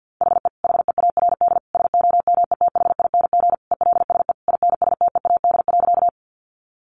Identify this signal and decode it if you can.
Morse code, heard as a keyed tone, about 50 wpm, 700 Hz. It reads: HE HEARD SOMETHING ELSE INSTEAD.
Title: Sentences from Top 800 Words 50wpm